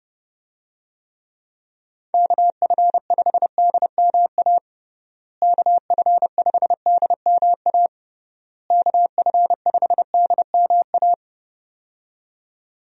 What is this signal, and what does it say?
Morse code, heard as a keyed tone, about 31 wpm, 700 Hz. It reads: KF5DMA KF5DMA KF5DMA
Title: USA Call Signs - 6 Characters - Repeated 3x - 30wpm